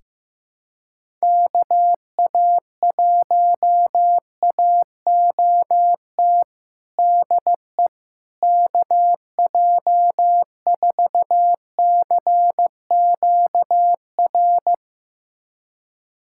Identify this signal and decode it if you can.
Morse code, heard as a keyed tone, about 15 wpm, 700 Hz. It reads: KA1AOT DE KJ4CQR
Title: USA Callsign DE Callsign 15wpm